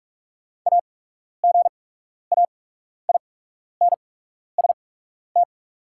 Morse code, heard as a keyed tone, about 44 wpm, spaced as for 12 wpm, 690 Hz. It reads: AGAINST
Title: 8x Character Spacing - Top 500 Words - 45wpm